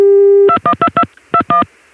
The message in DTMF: 212221